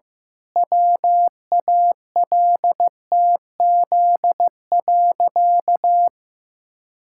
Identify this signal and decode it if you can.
Morse code, heard as a keyed tone, about 15 wpm, 700 Hz. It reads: WALTZ.
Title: Pangram Challenge v2 - 15wpm